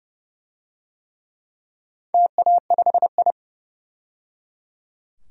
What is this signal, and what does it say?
Morse code, heard as a keyed tone, about 30 wpm, 700 Hz. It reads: TA5I
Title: DXCC-100 Call Signs - 4 Characters - 30wpm